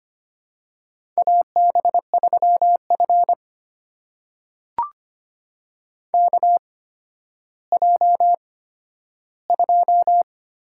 Morse code, heard as a keyed tone, about 25 wpm, spaced as for 12 wpm, 700 Hz. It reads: AB3F KJ2
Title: USA Call Signs - 4 Characters - 8x Character Spacing - 25wpm